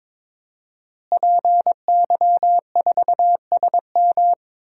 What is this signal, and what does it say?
Morse code, heard as a keyed tone, about 22 wpm, 700 Hz. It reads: PY4SM
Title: DXCC-100 Call Signs - 5 Characters - Repeated 3x - 22wpm